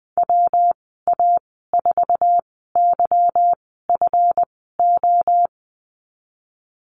Morse code, heard as a keyed tone, about 19 wpm, spaced as for 14 wpm, 700 Hz. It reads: WA4YFO
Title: 193 Lesson 31g - Letter v - Call Signs 20wpm